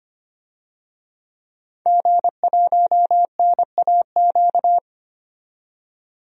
Morse code, heard as a keyed tone, about 25 wpm, 700 Hz. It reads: G1NAQ
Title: DXCC-100 Call Signs - 5 Characters - 25wpm